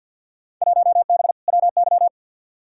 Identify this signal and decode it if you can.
Morse code, heard as a keyed tone, about 46 wpm, 700 Hz. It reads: 1B WY
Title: Field Day Exchanges Encore - 50wpm